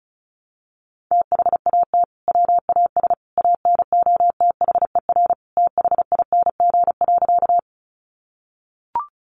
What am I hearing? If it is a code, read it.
Morse code, heard as a keyed tone, about 34 wpm, 700 Hz. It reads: THAT WAS ANOTHER THING.
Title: Sentences from Top 600 Words 35wpm